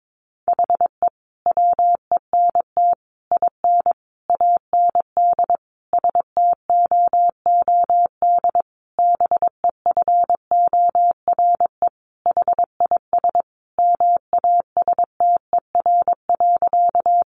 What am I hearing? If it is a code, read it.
Morse code, heard as a keyed tone, about 22 wpm, 700 Hz. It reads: HE WENT IN AND STOOD BEFORE HIS MASTER.
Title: Sentences from Top 1300 Words 22wpm